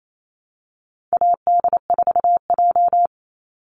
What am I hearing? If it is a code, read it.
Morse code, heard as a keyed tone, about 28 wpm, 700 Hz. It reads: AD4J
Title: Call Signs - 28wpm